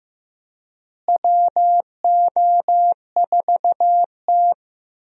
Morse code, heard as a keyed tone, about 15 wpm, 700 Hz. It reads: WO4T
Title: USA Call Signs - 4 Characters - Repeated 3x - 15wpm